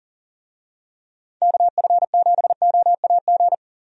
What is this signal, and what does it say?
Morse code, heard as a keyed tone, about 40 wpm, 700 Hz. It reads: KF7OAG
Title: USA Call Signs - 6 Characters - 40wpm